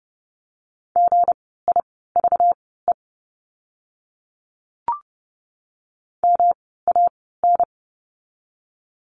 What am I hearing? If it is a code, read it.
Morse code, heard as a keyed tone, about 30 wpm, spaced as for 17 wpm, 700 Hz. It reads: GIVE MAN
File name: course / 30:10wpm / 237 Lesson 38f - Letter q - Words 30wpm